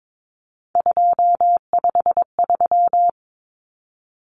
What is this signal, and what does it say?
Morse code, heard as a keyed tone, about 22 wpm, 700 Hz. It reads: 253
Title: Numbers - 3-Digit 22wpm